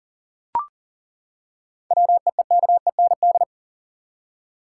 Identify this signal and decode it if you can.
Morse code, heard as a keyed tone, about 40 wpm, 700 Hz. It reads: WEEKEND
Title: Seven Letter Words - 40wpm